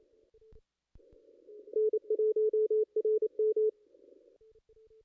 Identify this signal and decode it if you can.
Morse code, heard as a keyed tone, about 28 wpm, 440 Hz. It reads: N1RM
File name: contests / ss-cw-k5tr-op-n5rz -2200